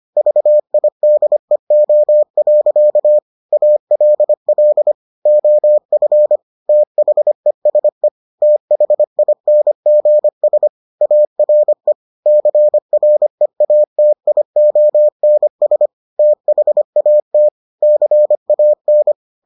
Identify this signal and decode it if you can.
Morse code, heard as a keyed tone, about 25 wpm, 600 Hz. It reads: VIDEO. ALL OF THESE THINGS ARE CREATIONS THAT CAN